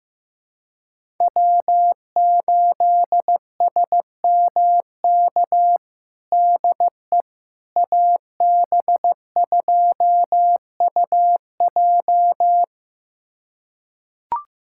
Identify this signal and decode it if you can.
Morse code, heard as a keyed tone, about 15 wpm, 700 Hz. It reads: W8SMK DE AB2UJ
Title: USA Callsign DE Callsign - 5 Characters - 15wpm